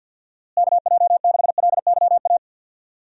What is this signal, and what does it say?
Morse code, heard as a keyed tone, about 52 wpm, 700 Hz. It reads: KJ6LYA
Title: USA Call Signs - 6 Characters - Repeated 2x - 50wpm